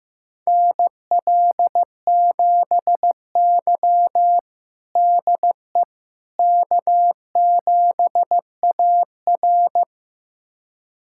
Morse code, heard as a keyed tone, about 15 wpm, 700 Hz. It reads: NL7Y DE K7AR
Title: USA Callsign DE Callsign - 4 Characters - 15wpm